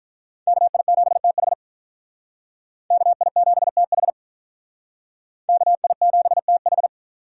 Morse code, heard as a keyed tone, about 46 wpm, 700 Hz. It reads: KI7TH KI7TH KI7TH